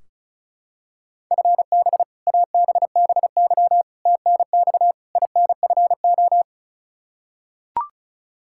Morse code, heard as a keyed tone, about 36 wpm, 700 Hz. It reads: FB ABBY TNX INFO
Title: Practice QSO - Exchange 3 - More Info - 35wpm